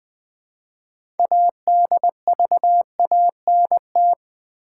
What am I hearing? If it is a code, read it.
Morse code, heard as a keyed tone, about 20 wpm, 700 Hz. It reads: ADVANT
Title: Build A Word - 3 Letters At A Time - 9-Letter Words - 20wpm